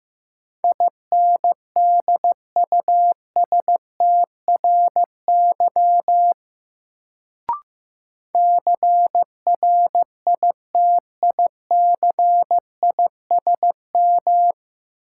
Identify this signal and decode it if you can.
Morse code, heard as a keyed tone, about 15 wpm, 700 Hz. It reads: INDUSTRY CRITICISM
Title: Sentences from Top 1300 Words - New Word Drill Alt - 15wpm